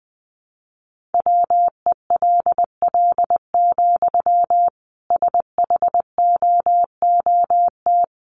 Morse code, heard as a keyed tone, about 20 wpm, 700 Hz. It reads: WELL, SHOOT
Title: Sentences from Top 1200 Words 20wpm